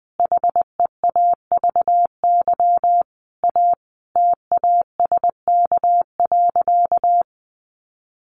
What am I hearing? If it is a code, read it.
Morse code, heard as a keyed tone, about 20 wpm, 700 Hz. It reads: HEAVY A TASK.
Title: Sentences from Top 1100 Words 20wpm